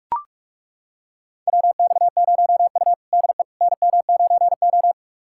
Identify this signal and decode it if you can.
Morse code, heard as a keyed tone, about 41 wpm, 700 Hz. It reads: WX0U DE NM9O
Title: USA Callsign DE Callsign - 4 Characters - 45wpm